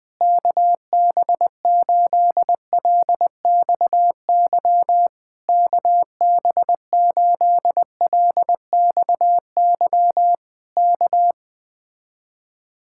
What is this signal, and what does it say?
Morse code, heard as a keyed tone, about 20 wpm, 700 Hz. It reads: KB8LXY KB8LXY K
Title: Speed-Racing Common QSO Phrases - 17wpm